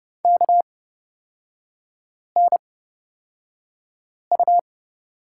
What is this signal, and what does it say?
Morse code, heard as a keyed tone, about 30 wpm, spaced as for 5 wpm, 700 Hz. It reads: KNU
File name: Lesson 36e - BT - Three Characters 30wpm